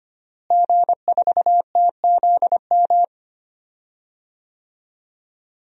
Morse code, heard as a keyed tone, about 25 wpm, 700 Hz. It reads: G4TZM
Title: DXCC-100 Call Signs - 5 Characters - Repeated 2x - 25wpm